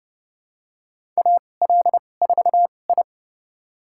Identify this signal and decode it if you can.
Morse code, heard as a keyed tone, about 30 wpm, spaced as for 22 wpm, 700 Hz. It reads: AL4I
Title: USA Call Signs - 4 Characters - 2x Character Spacing - 30wpm